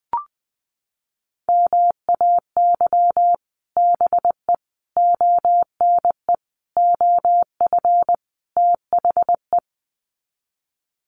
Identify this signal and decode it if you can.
Morse code, heard as a keyed tone, about 20 wpm, 700 Hz. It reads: MAY BE ONE OF THE
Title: Sets of 5 Words Encore2 20wpm